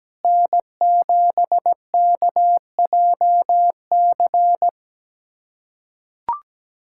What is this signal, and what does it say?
Morse code, heard as a keyed tone, about 17 wpm, 700 Hz. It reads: N7KJC
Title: USA Callsign DE Callsign - 5 Characters - 17wpm